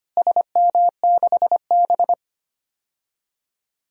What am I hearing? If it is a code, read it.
Morse code, heard as a keyed tone, about 25 wpm, 700 Hz. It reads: SM6B
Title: DXCC-100 Call Signs - 4 Characters - Repeated 2x - 25wpm